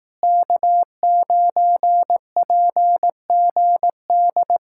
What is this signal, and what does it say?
Morse code, heard as a keyed tone, about 18 wpm, 700 Hz. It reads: K9PGD
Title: Speed-Racing Common QSO Phrases Encore - 15wpm